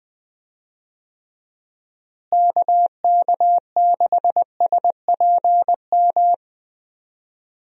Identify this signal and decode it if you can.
Morse code, heard as a keyed tone, about 20 wpm, 700 Hz. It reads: KK6SPM
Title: USA Call Signs - 6 Characters - Repeated 2x - 20wpm